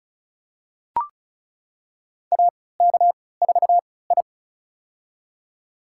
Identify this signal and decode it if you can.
Morse code, heard as a keyed tone, about 35 wpm, spaced as for 20 wpm, 700 Hz. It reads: AK4I
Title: USA Call Signs - 4 Characters - 3x Character Spacing - 35wpm